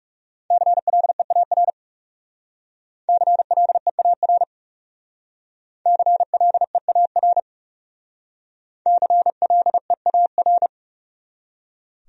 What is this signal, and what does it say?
Morse code, heard as a keyed tone, about 37 wpm, 700 Hz. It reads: CLEAR CLEAR CLEAR CLEAR
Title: Speed-Racing Top 500 Words 30wpm